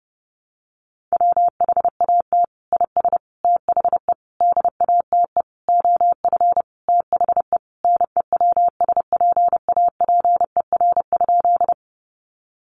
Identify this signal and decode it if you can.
Morse code, heard as a keyed tone, about 30 wpm, 700 Hz. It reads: WHAT IS THE DATE OF THE NEWSPAPER?
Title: Sentences from Top 1100 Words 30wpm